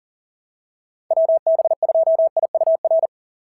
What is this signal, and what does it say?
Morse code, heard as a keyed tone, about 44 wpm, 690 Hz. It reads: WB2IUR
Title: SOTA QSO Practice 40wpm